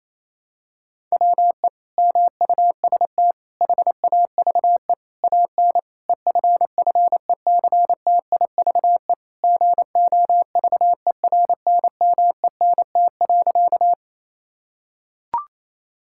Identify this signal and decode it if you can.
Morse code, heard as a keyed tone, about 28 wpm, 700 Hz. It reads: WE MUST HAVE AN EFFECTIVE GOVERNMENT.
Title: Sentences from Top 1000 Words - Review Part 6 - 28wpm